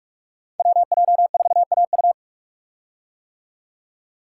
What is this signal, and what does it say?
Morse code, heard as a keyed tone, about 44 wpm, 700 Hz. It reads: WJ4AU
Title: Code Groups - 5 Letter:Numbers - 45wpm